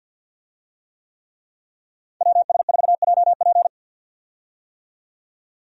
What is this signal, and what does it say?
Morse code, heard as a keyed tone, about 50 wpm, 700 Hz. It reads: WS4JP
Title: USA Call Signs - 5 Characters - 50wpm